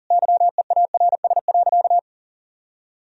Morse code, heard as a keyed tone, about 40 wpm, 700 Hz. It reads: YEARS.
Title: Sentences from Top 200 Words and Abbreviations 40wpm